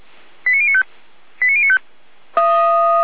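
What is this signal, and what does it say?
Five-tone selective call (ZVEI), heard as quick runs of five tones, tones of about 70 ms. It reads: 89095 89095
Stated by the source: ZVEI five-tone selective call received on an SDR.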